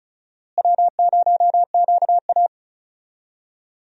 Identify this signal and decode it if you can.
Morse code, heard as a keyed tone, about 34 wpm, 700 Hz. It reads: W0QA
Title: USA Call Signs - 4 Characters - Repeated 3x - 35wpm